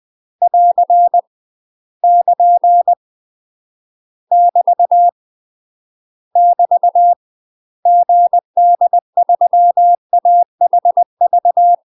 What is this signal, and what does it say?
Morse code, heard as a keyed tone, about 20 wpm, 700 Hz. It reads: <AR> <KN> <BT> <BT> GD3AHV
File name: UK MASTER QSOs TRAINING / 20WPM / UK MASTER QSOs 0035